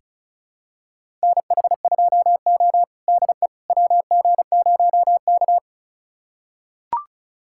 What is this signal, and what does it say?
Morse code, heard as a keyed tone, about 36 wpm, 700 Hz. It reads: NH2O DE WG0K
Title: USA Callsign DE Callsign - 4 Characters - 35wpm